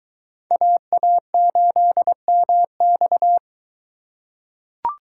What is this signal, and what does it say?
Morse code, heard as a keyed tone, about 21 wpm, 700 Hz. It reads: AA8MX